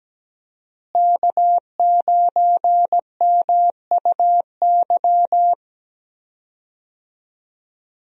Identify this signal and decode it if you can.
Morse code, heard as a keyed tone, about 17 wpm, 700 Hz. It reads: K9MUY
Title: USA Call Signs - 5 Characters - Repeated 3x - 17wpm